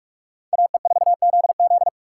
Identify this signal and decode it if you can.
Morse code, heard as a keyed tone, about 57 wpm, 700 Hz. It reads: AE4ZZ